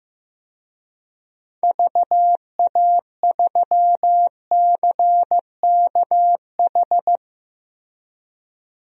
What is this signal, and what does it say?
Morse code, heard as a keyed tone, about 16 wpm, 700 Hz. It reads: VA3CKH
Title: DXCC-100 Call Signs - 6 Characters - Repeated 3x - 15wpm